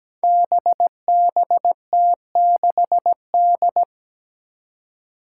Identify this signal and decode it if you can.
Morse code, heard as a keyed tone, about 17 wpm, 700 Hz. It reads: BBT6D
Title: Five Challenging Letter-Number Combinations 17wpm